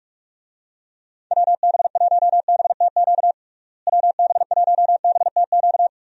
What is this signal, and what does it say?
Morse code, heard as a keyed tone, about 46 wpm, 700 Hz. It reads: WB1BTQ WB1BTQ
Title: USA Call Signs - 6 Characters - Repeated 3x - 45wpm